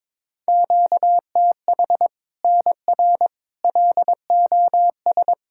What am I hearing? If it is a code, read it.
Morse code, heard as a keyed tone, about 21 wpm, 700 Hz. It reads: QTH NR LOS
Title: Practice QSO - Exchange 1 - Beginning - 22wpm